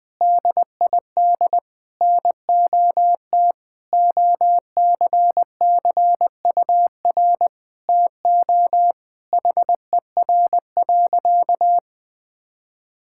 Morse code, heard as a keyed tone, about 21 wpm, 700 Hz. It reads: DID NOT OCCUR TO HER.
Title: Sentences from Top 1000 Words - Review Part 6 - 20wpm